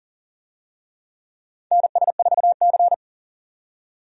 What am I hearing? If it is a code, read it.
Morse code, heard as a keyed tone, about 40 wpm, 700 Hz. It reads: NS4C
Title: USA Call Signs - 4 Characters - 40wpm